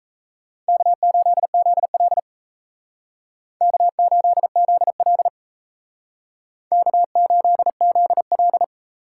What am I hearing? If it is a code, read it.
Morse code, heard as a keyed tone, about 35 wpm, 700 Hz. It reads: K8ZL K8ZL K8ZL
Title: Speed Racing - USA 1x2 Call Signs 28wpm